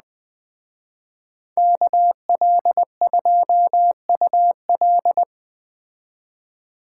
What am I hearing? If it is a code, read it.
Morse code, heard as a keyed tone, about 20 wpm, 700 Hz. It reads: KL2UL
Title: Speed Racing - USA Call Signs 17wpm